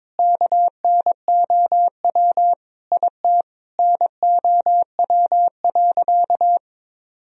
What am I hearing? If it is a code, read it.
Morse code, heard as a keyed tone, about 22 wpm, 700 Hz. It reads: KNOW IT NOW.
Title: Sentences from Top 1000 Words - Review Part 4 - 22wpm